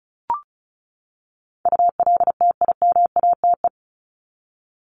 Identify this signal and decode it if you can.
Morse code, heard as a keyed tone, about 34 wpm, 700 Hz. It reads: ULTIMATE
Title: LY Suffix - Top 100 Words with Base Words 35wpm